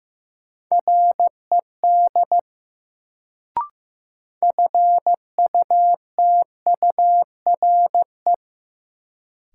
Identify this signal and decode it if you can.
Morse code, heard as a keyed tone, about 15 wpm, 700 Hz. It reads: RED FUTURE
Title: Sentences from Top 700 Words - New Word Drill - 15wpm